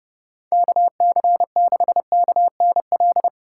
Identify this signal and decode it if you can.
Morse code, heard as a keyed tone, about 31 wpm, 700 Hz. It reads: KC6KNL
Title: USA Call Signs - 6 Characters - 4x Character Spacing - 30wpm